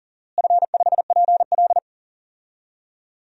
Morse code, heard as a keyed tone, about 40 wpm, 700 Hz. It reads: F5PL
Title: DXCC-100 Call Signs - 4 Characters - Repeated 2x - 40wpm